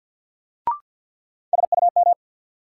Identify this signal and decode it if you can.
Morse code, heard as a keyed tone, about 49 wpm, 700 Hz. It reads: SUM
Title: Sentences from Top 1000 Words - Review Part 6 - New Word Drill - 50wpm